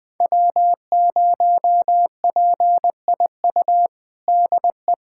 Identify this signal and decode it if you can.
Morse code, heard as a keyed tone, about 21 wpm, 700 Hz. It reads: W0PIU DE
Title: USA Callsign DE Callsign - 5 Characters - 20wpm